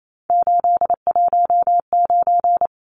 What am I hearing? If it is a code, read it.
Morse code, heard as a keyed tone, about 28 wpm, 700 Hz. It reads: 819
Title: Numbers - 3-Digit 28wpm